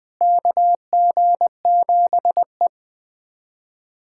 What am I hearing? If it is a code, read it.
Morse code, heard as a keyed tone, about 20 wpm, 700 Hz. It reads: KG7E
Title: USA Callsign DE Callsign - 4 Characters - 20wpm